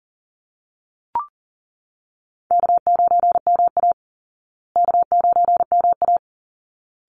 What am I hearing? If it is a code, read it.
Morse code, heard as a keyed tone, about 37 wpm, 700 Hz. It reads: K9MA K9MA
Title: Easy Call Signs Repeated 2x - 40wpm